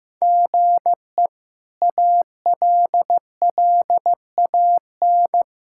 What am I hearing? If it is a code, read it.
Morse code, heard as a keyed tone, about 15 wpm, 700 Hz. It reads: GE ALLAN
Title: K1USN SST QSO Practice - 15wpm